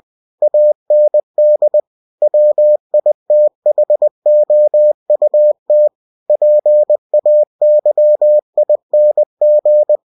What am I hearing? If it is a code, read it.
Morse code, heard as a keyed tone, about 20 wpm, 600 Hz. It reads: AND WITHOUT PAYING
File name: cw-books / winnie-the-pooh / winnie-20wpm-page-0112